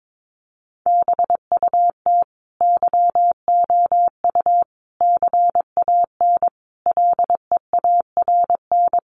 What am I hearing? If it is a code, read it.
Morse code, heard as a keyed tone, about 22 wpm, 700 Hz. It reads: BUT YOU CAN LEARN
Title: Sentences from Top 1500 Words 22wpm